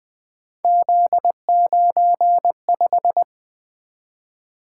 Morse code, heard as a keyed tone, about 20 wpm, 700 Hz. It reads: Z95
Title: Three Letter-Number Combinations - 20wpm